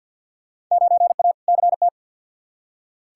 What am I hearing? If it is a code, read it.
Morse code, heard as a keyed tone, about 53 wpm, 700 Hz. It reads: 9A CT
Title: Field Day Exchanges Encore - 50wpm